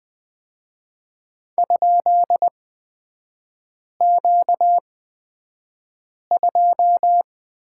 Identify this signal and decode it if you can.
Morse code, heard as a keyed tone, about 20 wpm, spaced as for 5 wpm, 700 Hz. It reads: ?Q2
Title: 236 Lesson 38e - Letter q - Three Characters 20wpm